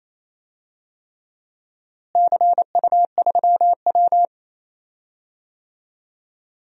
Morse code, heard as a keyed tone, about 28 wpm, 700 Hz. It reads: CU3W
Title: DXCC-100 Call Signs - 4 Characters - Repeated 3x - 28wpm